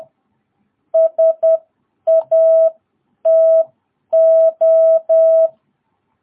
Morse code, heard as a keyed tone, about 10 wpm, spaced as for 8 wpm, 660 Hz. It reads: SATO
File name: jh8cbh / 2022cw